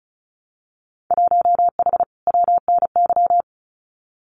Morse code, heard as a keyed tone, about 36 wpm, 700 Hz. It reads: 1H WNY
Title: Winter Field Day Exchanges 35wpm